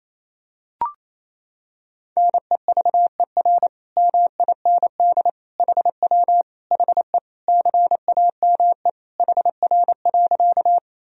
Morse code, heard as a keyed tone, about 28 wpm, 700 Hz. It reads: NEVER MIND HW HE CAME HR.
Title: Sentences from Top 500 Words and Abbreviations 28wpm